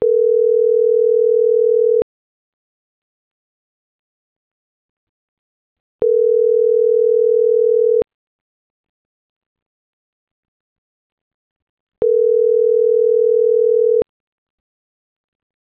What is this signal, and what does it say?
Ringback tone (North America), 440 + 480 Hz.
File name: CENTRAL GENERAL / out america.mejia